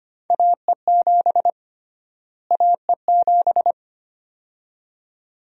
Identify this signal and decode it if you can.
Morse code, heard as a keyed tone, about 25 wpm, 700 Hz. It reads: AE7 AE7
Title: USA Call Sign Prefixes - Repeated 2x - 25wpm